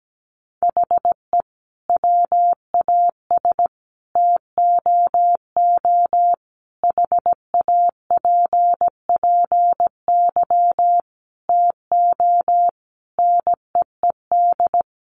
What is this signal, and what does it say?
Morse code, heard as a keyed tone, about 17 wpm, 700 Hz. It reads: HE WAS TOO HAPPY TO NEED
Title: Sentences from Top 1000 Words 17wpm